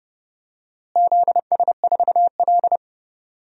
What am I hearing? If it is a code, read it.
Morse code, heard as a keyed tone, about 30 wpm, 700 Hz. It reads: ZS4L
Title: DXCC-100 Call Signs - 4 Characters - Repeated 2x - 30wpm